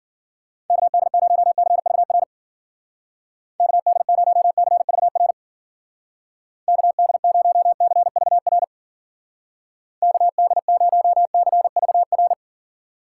Morse code, heard as a keyed tone, about 50 wpm, 700 Hz. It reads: KD0CVR KD0CVR KD0CVR KD0CVR